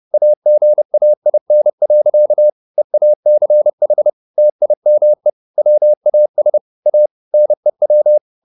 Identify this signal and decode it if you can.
Morse code, heard as a keyed tone, about 30 wpm, 600 Hz. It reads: AGAIN. EACH TIME WAS A NEW
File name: the old man and the sea / the old man and the sea 1 30wpm